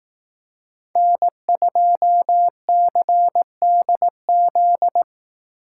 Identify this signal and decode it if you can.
Morse code, heard as a keyed tone, about 18 wpm, 700 Hz. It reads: N2CDZ
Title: Speed Racing - USA 1x3 Call Signs 15wpm